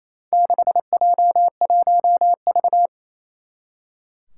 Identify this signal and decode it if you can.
Morse code, heard as a keyed tone, about 28 wpm, 700 Hz. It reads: 6J1V
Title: Four Challenging Letter-Number Combinations 28wpm